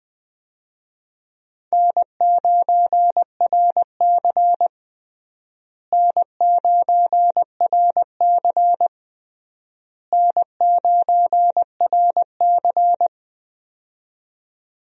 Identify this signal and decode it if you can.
Morse code, heard as a keyed tone, about 20 wpm, 700 Hz. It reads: N9RC N9RC N9RC
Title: USA Call Signs - 4 Characters - Repeated 3x - 20wpm